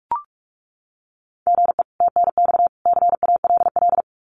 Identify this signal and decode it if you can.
Morse code, heard as a keyed tone, about 44 wpm, 700 Hz. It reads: GE TNX CALL
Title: Common QSO Phrases 45wpm